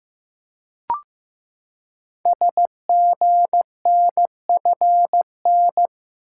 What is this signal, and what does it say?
Morse code, heard as a keyed tone, about 15 wpm, 700 Hz. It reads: SGNFN
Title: Code Groups - 5 Letters - 15wpm